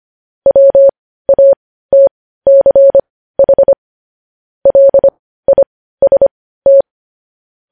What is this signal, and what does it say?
Morse code, heard as a keyed tone, about 25 wpm, spaced as for 15 wpm, 550 Hz. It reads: WATCH LIST